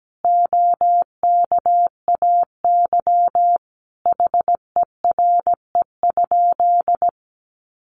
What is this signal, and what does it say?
Morse code, heard as a keyed tone, about 17 wpm, 700 Hz. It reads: OKAY HERE?
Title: A Sound Of Thunder 17wpm